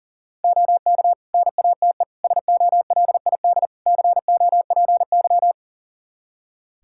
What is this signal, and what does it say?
Morse code, heard as a keyed tone, about 37 wpm, 700 Hz. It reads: OK NATE SOLID COPY
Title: Practice QSO - Exchange 3 - More Info - 40wpm